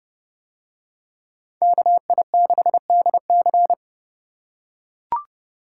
Morse code, heard as a keyed tone, about 29 wpm, 700 Hz. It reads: KI6DC